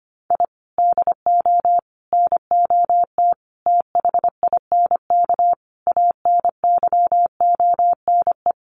Morse code, heard as a keyed tone, about 25 wpm, 700 Hz. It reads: I DO NOT THINK ANYONE
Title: Sets of 5 Words Encore 25wpm